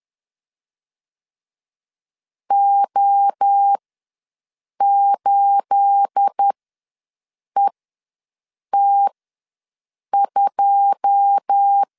Morse code, heard as a keyed tone, about 10 wpm, spaced as for 6 wpm, 800 Hz. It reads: O8ET2
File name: LEZIONI / 9 28 ws 10wpm 4124